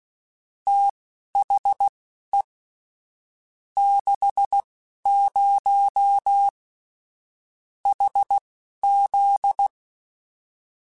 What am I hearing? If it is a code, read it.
Morse code, heard as a keyed tone, about 16 wpm, spaced as for 11 wpm, 800 Hz. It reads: THE 60 HZ
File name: Technical Subjects / Technical Subjects-mp3s / Sw Pwr Sup @10